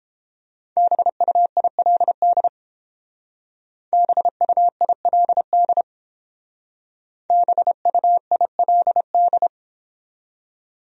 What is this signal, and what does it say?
Morse code, heard as a keyed tone, about 31 wpm, 700 Hz. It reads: BUILD BUILD BUILD